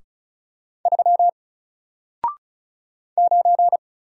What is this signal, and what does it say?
Morse code, heard as a keyed tone, about 36 wpm, 700 Hz. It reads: 3 9
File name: Single-Letter-Number-35wpm